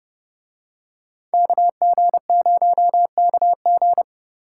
Morse code, heard as a keyed tone, about 29 wpm, 700 Hz. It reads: KG0KG